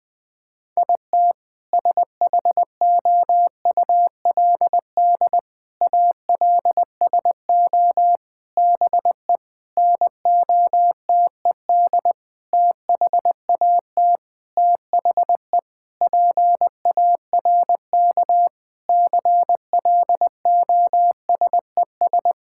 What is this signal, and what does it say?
Morse code, heard as a keyed tone, about 20 wpm, 700 Hz. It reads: IT SHOULD ALSO BE NOTED THAT THE PARK CLOSES
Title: Sentences with 6-Word Phrases - 20wpm